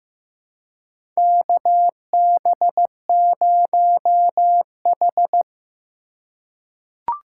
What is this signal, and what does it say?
Morse code, heard as a keyed tone, about 15 wpm, 700 Hz. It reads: KB0H
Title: USA Call Signs - 3x Character Spacing - 15wpm